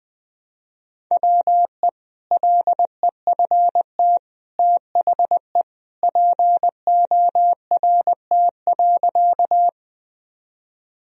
Morse code, heard as a keyed tone, about 19 wpm, 700 Hz. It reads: WE LEFT THE PORT.
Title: Sentences from Top 1000 Words - Review Part 1 - 20wpm